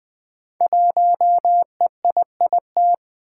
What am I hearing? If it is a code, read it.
Morse code, heard as a keyed tone, about 20 wpm, 700 Hz. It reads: 1EIIT
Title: Five Challenging Letter-Number Combinations 20wpm